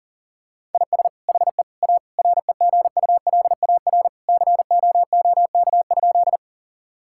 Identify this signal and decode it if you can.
Morse code, heard as a keyed tone, about 40 wpm, 690 Hz. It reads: IS HE A REGULAR COOK?
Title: Sentences from Top 1000 Words - Review Part 6 - 40wpm